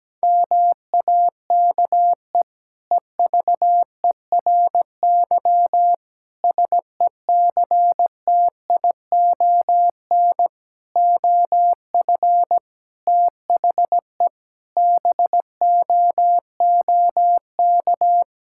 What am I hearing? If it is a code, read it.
Morse code, heard as a keyed tone, about 17 wpm, 700 Hz. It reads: MAKE EVERY SECTION OF THE BOOK
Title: Sentences from Top 1000 Words - Review Part 5 - 17wpm